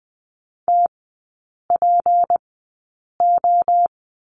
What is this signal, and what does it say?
Morse code, heard as a keyed tone, about 20 wpm, spaced as for 8 wpm, 700 Hz. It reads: TPO
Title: 218 Lesson 35e - Number 0 - Three Characters 20wpm